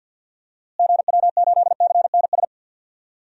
Morse code, heard as a keyed tone, about 45 wpm, 700 Hz. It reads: GW8CNS
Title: DXCC-100 Call Signs - 6 Characters - Repeated 2x - 50wpm